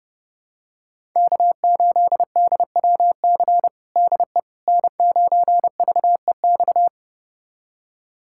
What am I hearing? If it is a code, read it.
Morse code, heard as a keyed tone, about 32 wpm, 700 Hz. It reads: K8DWC DE N9VEX
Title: USA Callsign DE Callsign - 5 Characters - 30wpm